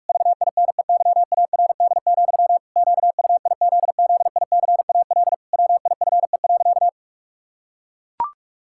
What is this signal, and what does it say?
Morse code, heard as a keyed tone, about 41 wpm, 700 Hz. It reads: VINEYARD, QUIZZICAL WIFE.
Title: Pangram Challenge v2 - 45wpm